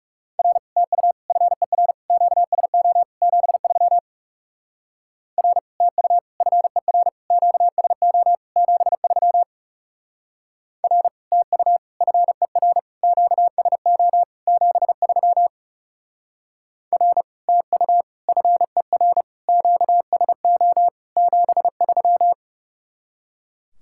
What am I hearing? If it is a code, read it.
Morse code, heard as a keyed tone, about 36 wpm, 700 Hz. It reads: R TU FER QSO 73 R TU FER QSO 73 R TU FER QSO 73 R TU FER QSO 73
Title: Speed-Racing Common QSO Phrases - 30wpm